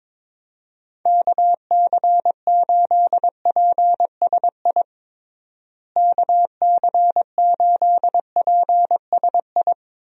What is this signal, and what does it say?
Morse code, heard as a keyed tone, about 22 wpm, 700 Hz. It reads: KC8PSI KC8PSI
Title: USA Call Signs - 6 Characters - Repeated 3x - 22wpm